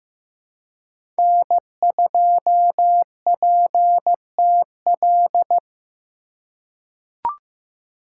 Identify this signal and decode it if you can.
Morse code, heard as a keyed tone, about 15 wpm, 700 Hz. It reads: N2PTL
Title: USA Call Signs - 5 Characters - 4x Character Spacing - 15wpm